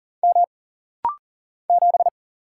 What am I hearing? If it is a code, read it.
Morse code, heard as a keyed tone, about 40 wpm, 700 Hz. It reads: M 7
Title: Single Letter-Number Alt - Rapid-Fire 40wpm